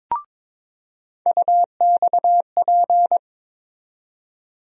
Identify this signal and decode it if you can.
Morse code, heard as a keyed tone, about 22 wpm, 700 Hz. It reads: UXP
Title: USA Call Sign Postfixes - 3 Letters 22wpm